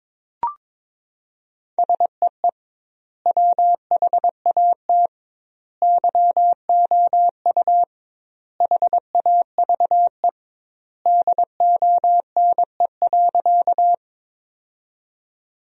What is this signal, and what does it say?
Morse code, heard as a keyed tone, about 22 wpm, 700 Hz. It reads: SEE WHAT YOU HAVE DONE.
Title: 2x Word-Spacing - Sentences from Top 500 Words - 22wpm